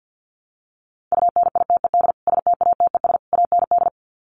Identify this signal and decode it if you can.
Morse code, heard as a keyed tone, about 50 wpm, 700 Hz. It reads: UNITED STATES AND
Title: Sets of 3 Words 50wpm